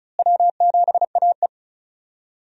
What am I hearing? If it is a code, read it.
Morse code, heard as a keyed tone, about 36 wpm, 700 Hz. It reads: W7AE
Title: USA Call Signs - 4 Characters - Repeated 2x - 35wpm